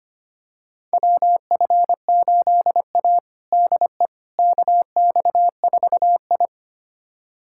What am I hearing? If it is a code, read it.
Morse code, heard as a keyed tone, about 25 wpm, 700 Hz. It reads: WF8A DE KX4I
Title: USA Callsign DE Callsign - 4 Characters - 25wpm